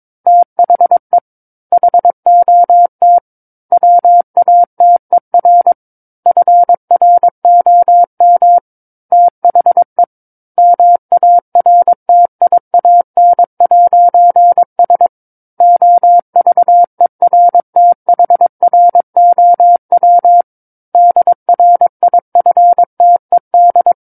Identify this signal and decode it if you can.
Morse code, heard as a keyed tone, about 22 wpm, 700 Hz. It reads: THE HOT WATER FROM THE MARTIAN'S OVERTHROW DRIFTED
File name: Book One / 13-Chapter 13 - 22 WPM